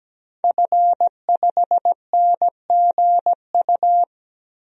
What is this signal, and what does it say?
Morse code, heard as a keyed tone, about 17 wpm, 700 Hz. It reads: F5NGU
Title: DXCC-100 Call Signs - 5 Characters - 17wpm